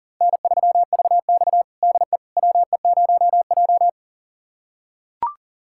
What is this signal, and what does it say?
Morse code, heard as a keyed tone, about 40 wpm, 700 Hz. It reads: N3VX DE WE0J
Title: USA Callsign DE Callsign - 4 Characters - 40wpm